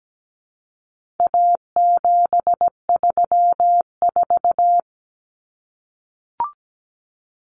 Morse code, heard as a keyed tone, about 17 wpm, 700 Hz. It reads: A734
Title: Common Cut Numbers - 4-Digit 17wpm